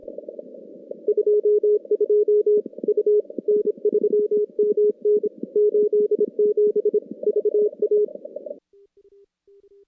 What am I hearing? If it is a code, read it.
Morse code, heard as a keyed tone, about 30 wpm, 420 Hz. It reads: 22 U N3MN 87 VA
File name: contests / CW-SS-k5tr- -2100